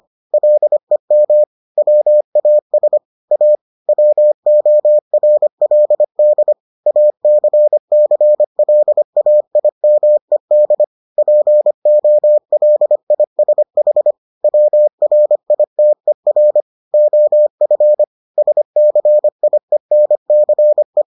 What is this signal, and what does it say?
Morse code, heard as a keyed tone, about 25 wpm, 600 Hz. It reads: LEM WAS A WORLD ACCLAIMED POLISH WRITER OF SCIENCE